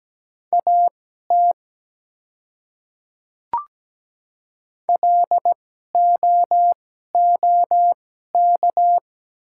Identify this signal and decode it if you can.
Morse code, heard as a keyed tone, about 17 wpm, spaced as for 12 wpm, 700 Hz. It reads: AT LOOK
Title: 2x Character Spacing - Top 200 Words - 17wpm